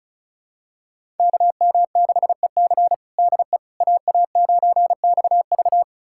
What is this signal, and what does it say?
Morse code, heard as a keyed tone, about 36 wpm, 700 Hz. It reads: KM6EC DE AA9XV
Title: USA Callsign DE Callsign - 5 Characters - 35wpm